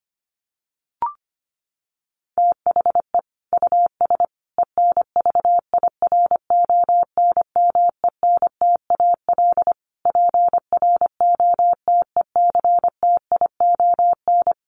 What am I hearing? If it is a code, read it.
Morse code, heard as a keyed tone, about 25 wpm, 700 Hz. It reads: THE US ENVIRONMENTAL PROTECTION
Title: Sets of 5 Words Encore2 25wpm